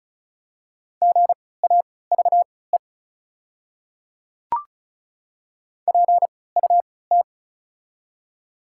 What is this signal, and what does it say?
Morse code, heard as a keyed tone, about 36 wpm, spaced as for 20 wpm, 690 Hz. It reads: GAVE PUT
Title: 3x Character Spacing - Top 500 Words - 35wpm